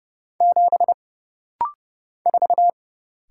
Morse code, heard as a keyed tone, about 30 wpm, 700 Hz. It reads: 7 4
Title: Single Letter-Number Alt - Rapid-Fire 30wpm